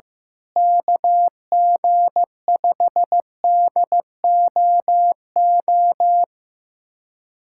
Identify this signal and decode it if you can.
Morse code, heard as a keyed tone, about 15 wpm, 700 Hz. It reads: KG5DOO
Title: USA Callsign DE Callsign - 6 Characters - 15wpm